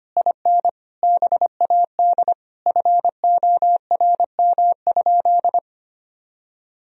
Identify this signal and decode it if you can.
Morse code, heard as a keyed tone, about 25 wpm, 700 Hz. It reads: IN BAD FORM?
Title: Sentences from Top 1000 Words - Review Part 1 - 25wpm